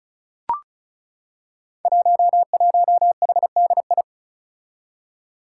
Morse code, heard as a keyed tone, about 36 wpm, 700 Hz. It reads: 11HDI
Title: Five Challenging Letter-Number Combinations 35wpm